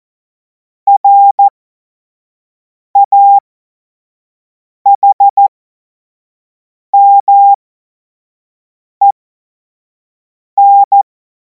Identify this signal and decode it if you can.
Morse code, heard as a keyed tone, about 14 wpm, spaced as for 5 wpm, 800 Hz. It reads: RAHMEN
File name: Morsekurs / Witze / Witz 10. Februar 2026 25 F70